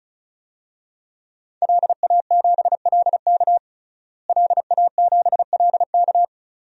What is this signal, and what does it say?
Morse code, heard as a keyed tone, about 34 wpm, 700 Hz. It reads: LA7LK LA7LK
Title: DXCC-100 Call Signs - 5 Characters - Repeated 2x - 35wpm